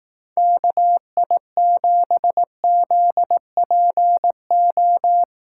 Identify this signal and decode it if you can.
Morse code, heard as a keyed tone, about 18 wpm, 700 Hz. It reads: KI7ZPO
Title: Speed Racing - USA Call Signs 15wpm